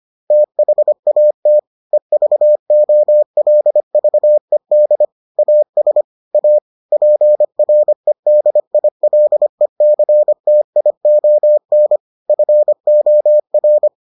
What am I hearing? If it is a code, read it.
Morse code, heard as a keyed tone, about 25 wpm, 600 Hz. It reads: THAT EVOLVED AS A PREDILECTION FOR